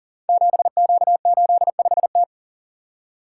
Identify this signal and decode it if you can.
Morse code, heard as a keyed tone, about 38 wpm, 700 Hz. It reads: 7Q85T
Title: Code Groups - 5 Letter:Numbers - 40wpm